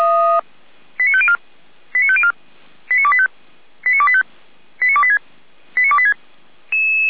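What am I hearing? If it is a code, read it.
Five-tone selective call (ZVEI), heard as quick runs of five tones, tones of about 70 ms. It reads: 89504 89504 89285 89285 89286 89286
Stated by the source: ZVEI five-tone call 89280 received by SDR.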